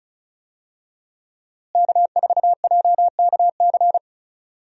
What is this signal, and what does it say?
Morse code, heard as a keyed tone, about 36 wpm, 700 Hz. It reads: K4JKC
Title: USA Call Signs - 5 Characters - Repeated 3x - 35wpm